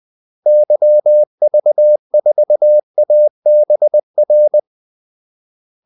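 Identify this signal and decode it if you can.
Morse code, heard as a keyed tone, about 20 wpm, 600 Hz. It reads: YV4ABR
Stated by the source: Morse call-sign practice at 20 wpm.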